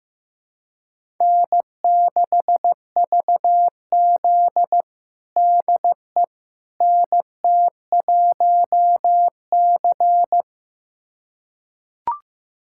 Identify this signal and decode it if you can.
Morse code, heard as a keyed tone, about 15 wpm, 700 Hz. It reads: N6VZ DE NT1C
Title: USA Callsign DE Callsign - 4 Characters - 15wpm